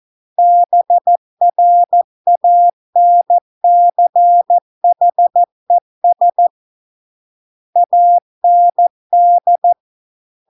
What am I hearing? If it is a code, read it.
Morse code, heard as a keyed tone, about 14 wpm, 700 Hz. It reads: BRANCHES AND